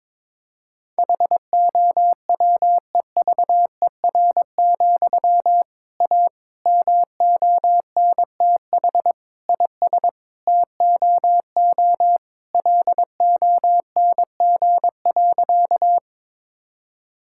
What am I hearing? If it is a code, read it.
Morse code, heard as a keyed tone, about 21 wpm, 700 Hz. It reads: HOWEVER, A MONTH IS TOO LONG.
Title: Sentences from Top 1000 Words - Review Part 5 - 22wpm